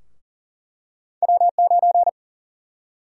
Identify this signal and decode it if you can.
Morse code, heard as a keyed tone, about 40 wpm, 700 Hz. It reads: W9
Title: USA Call Sign Prefixes - 40wpm